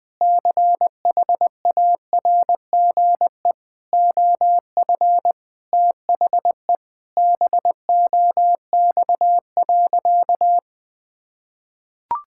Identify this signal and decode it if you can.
Morse code, heard as a keyed tone, about 20 wpm, 700 Hz. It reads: CHARGE OF THE BOX.
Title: Sentences from Top 1000 Words - Review Part 6 - 20wpm